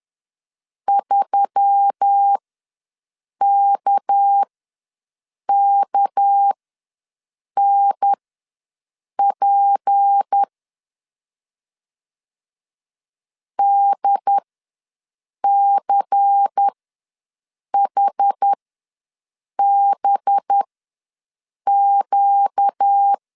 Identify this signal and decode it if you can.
Morse code, heard as a keyed tone, about 10 wpm, spaced as for 6 wpm, 800 Hz. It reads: 3KKNP DCHBQ
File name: LEZIONI / 6 PKX0 / 6 PK ws 10wpm 3745